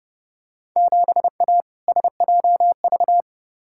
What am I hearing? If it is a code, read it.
Morse code, heard as a keyed tone, about 30 wpm, 700 Hz. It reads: 7A SJV
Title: Field Day Exchanges Encore - 30wpm